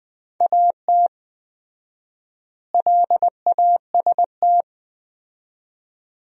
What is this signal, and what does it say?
Morse code, heard as a keyed tone, about 20 wpm, 700 Hz. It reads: AT LAST
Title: 4x Word-Spacing - Sentences from Top 300 Words - 20wpm